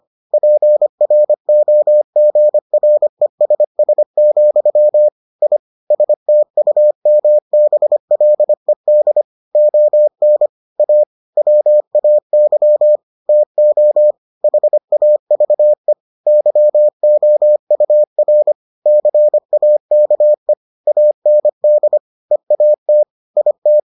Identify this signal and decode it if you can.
Morse code, heard as a keyed tone, about 25 wpm, 600 Hz. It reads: PROGRESS, I STUMBLED ON A WAY TO HAVE YOUR CAKE AND EAT IT